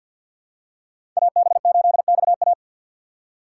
Morse code, heard as a keyed tone, about 46 wpm, 700 Hz. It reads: AB8XA
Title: USA Call Signs - 5 Characters - 3x Character Spacing - 50wpm